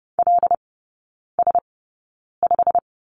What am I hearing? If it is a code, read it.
Morse code, heard as a keyed tone, about 30 wpm, spaced as for 9 wpm, 700 Hz. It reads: LS5
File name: course / 30:10wpm / 087 Lesson 15e - Number 5 - Three Characters 30wpm